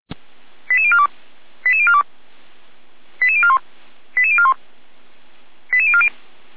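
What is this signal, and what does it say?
Five-tone selective call (ZVEI), heard as quick runs of five tones, tones of about 70 ms. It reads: 88042 88042 88041 88041 88040
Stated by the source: ZVEI five-tone call received by SDR.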